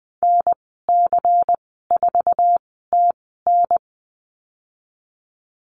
Morse code, heard as a keyed tone, about 20 wpm, spaced as for 15 wpm, 700 Hz. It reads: NC4TN
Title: 154 Lesson 25g - Letter p - Call Signs 20wpm